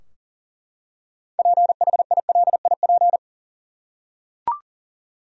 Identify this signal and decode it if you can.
Morse code, heard as a keyed tone, about 40 wpm, 700 Hz. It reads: PHILIP
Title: USA Common Names - 40wpm